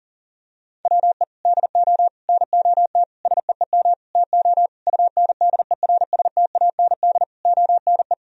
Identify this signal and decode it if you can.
Morse code, heard as a keyed tone, about 40 wpm, 700 Hz. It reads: WE DO NOT SEEM TO UNDERSTAND ONE
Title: Sentences from Top 1000 Words - Review Part 1 - 40wpm